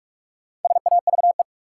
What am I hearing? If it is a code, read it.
Morse code, heard as a keyed tone, about 45 wpm, 700 Hz. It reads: SAVE
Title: Sentences from Top 1100 Words - New Word Drill Alt - 45wpm